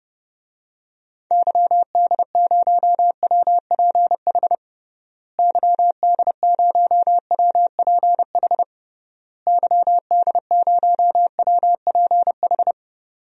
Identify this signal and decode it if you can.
Morse code, heard as a keyed tone, about 30 wpm, 700 Hz. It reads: YD0WPH YD0WPH YD0WPH
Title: DXCC-100 Call Signs - 6 Characters - Repeated 3x - 30wpm